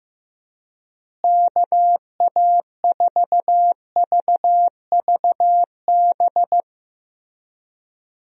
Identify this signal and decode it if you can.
Morse code, heard as a keyed tone, about 15 wpm, 700 Hz. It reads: KA4VVB
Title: USA Call Signs - 6 Characters - Repeated 2x - 15wpm